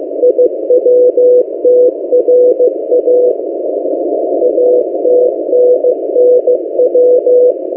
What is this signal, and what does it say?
Morse code, heard as a keyed tone, about 15 wpm, 500 Hz. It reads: IWTRA ATNNW
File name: M08a- -1355utc-8096khz